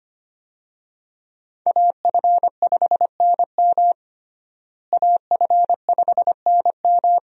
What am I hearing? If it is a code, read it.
Morse code, heard as a keyed tone, about 25 wpm, 700 Hz. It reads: AF5NM AF5NM
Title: USA Call Signs - 5 Characters - Repeated 2x - 25wpm